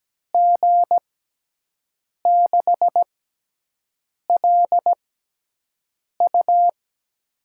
Morse code, heard as a keyed tone, about 17 wpm, 700 Hz. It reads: G 6 L U
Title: USA Call Signs - 5 Characters - 6x Character Spacing - 17wpm